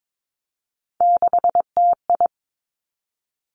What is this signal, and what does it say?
Morse code, heard as a keyed tone, about 22 wpm, 700 Hz. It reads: 6TI
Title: Three Challenging Letter-Number Combinations 22wpm